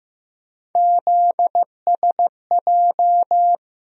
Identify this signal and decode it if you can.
Morse code, heard as a keyed tone, about 15 wpm, 700 Hz. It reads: ZSJ
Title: USA Call Sign Postfixes - 3 Letters 15wpm